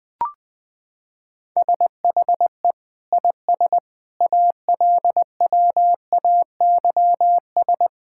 Morse code, heard as a keyed tone, about 20 wpm, 700 Hz. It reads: SHE IS ALWAYS
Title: Sentences from Top 1000 Words - Review Part 5 - 20wpm